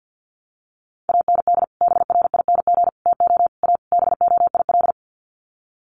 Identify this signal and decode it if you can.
Morse code, heard as a keyed tone, about 48 wpm, 700 Hz. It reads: AND BRING TO A BOIL
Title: Sets of 5 Words Encore 50wpm